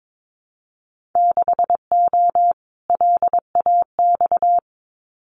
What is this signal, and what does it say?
Morse code, heard as a keyed tone, about 22 wpm, 700 Hz. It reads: 6O LAX
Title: Winter Field Day Exchanges Encore 22wpm